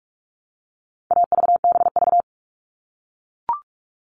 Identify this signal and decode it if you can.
Morse code, heard as a keyed tone, about 44 wpm, 700 Hz. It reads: AVBV
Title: All Cut Numbers - 4-Digit 45wpm